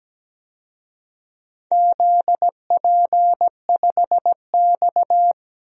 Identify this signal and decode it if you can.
Morse code, heard as a keyed tone, about 17 wpm, 700 Hz. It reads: ZP5X
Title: DXCC-100 Call Signs - 4 Characters - 17wpm